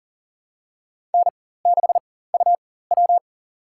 Morse code, heard as a keyed tone, about 40 wpm, spaced as for 19 wpm, 700 Hz. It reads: N6UW
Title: USA Call Signs - 4 Characters - 4x Character Spacing - 40wpm